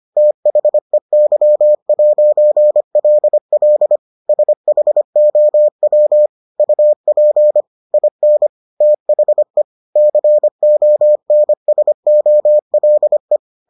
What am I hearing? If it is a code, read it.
Morse code, heard as a keyed tone, about 25 wpm, 600 Hz. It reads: THEY'LL SHOW UP IN THE CONSOLE